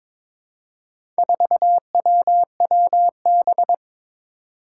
Morse code, heard as a keyed tone, about 22 wpm, 700 Hz. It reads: 4WWB
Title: Four Challenging Letter-Number Combinations 22wpm